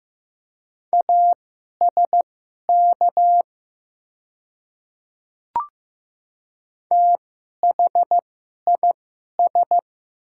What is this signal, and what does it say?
Morse code, heard as a keyed tone, about 15 wpm, spaced as for 11 wpm, 700 Hz. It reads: ASK THIS
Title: 2x Character Spacing - Top 200 Words - 15wpm